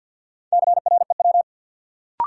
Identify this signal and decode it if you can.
Morse code, heard as a keyed tone, about 53 wpm, 700 Hz. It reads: CREW
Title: Sentences from Top 1500 Words - New Word Drill - 50wpm